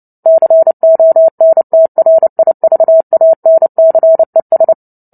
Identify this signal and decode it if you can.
Morse code, heard as a keyed tone, about 30 wpm, 650 Hz. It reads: CONTRIVANCES
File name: Book Two / 03-Chapter 3 - 29 WPM